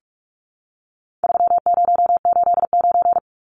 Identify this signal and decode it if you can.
Morse code, heard as a keyed tone, about 44 wpm, 700 Hz. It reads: 3089
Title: Numbers - 4-Digit 45wpm